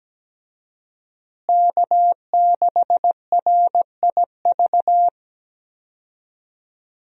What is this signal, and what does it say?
Morse code, heard as a keyed tone, about 17 wpm, 700 Hz. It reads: K6RIV
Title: USA Call Signs - 5 Characters - Repeated 3x - 17wpm